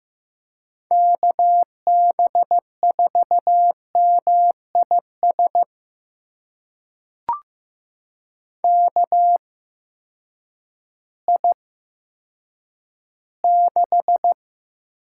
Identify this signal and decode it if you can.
Morse code, heard as a keyed tone, about 15 wpm, spaced as for 9 wpm, 700 Hz. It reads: KB4MIS KI6
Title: USA Call Signs - 6 Characters - 8x Character Spacing - 15wpm